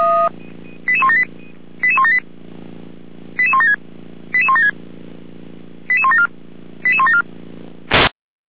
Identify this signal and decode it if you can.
Five-tone selective call (ZVEI), heard as quick runs of five tones, tones of about 70 ms. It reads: 88178 88178 88176 88176 88174 88174